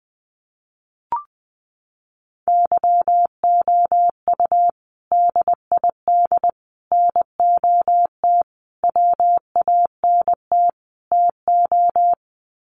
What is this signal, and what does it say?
Morse code, heard as a keyed tone, about 20 wpm, 700 Hz. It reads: YOU DID NOT WANT TO
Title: Sets of 5 Words Encore2 20wpm